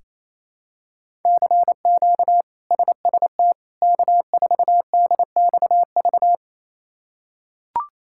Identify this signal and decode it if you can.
Morse code, heard as a keyed tone, about 28 wpm, 700 Hz. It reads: CQ SST K4DXV
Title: K1USN SST QSO Practice - 28wpm